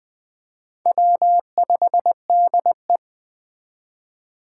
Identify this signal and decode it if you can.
Morse code, heard as a keyed tone, about 20 wpm, 700 Hz. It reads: W5DE
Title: Speed Racing - USA Call Signs 17wpm